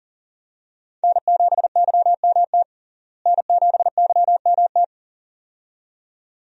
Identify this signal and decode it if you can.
Morse code, heard as a keyed tone, about 40 wpm, 700 Hz. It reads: N7YMT N7YMT
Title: USA Call Signs - 5 Characters - Repeated 2x - 40wpm